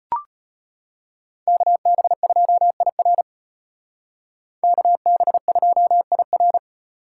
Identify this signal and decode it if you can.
Morse code, heard as a keyed tone, about 32 wpm, 700 Hz. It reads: KB2IR KB2IR